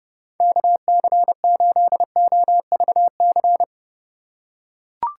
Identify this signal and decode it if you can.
Morse code, heard as a keyed tone, about 30 wpm, 700 Hz. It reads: KC8OVC
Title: USA Call Signs - 6 Characters - 4x Character Spacing - 30wpm